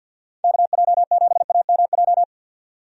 Morse code, heard as a keyed tone, about 51 wpm, 700 Hz. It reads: KJ7AMJ
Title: USA Call Signs - 6 Characters - 50wpm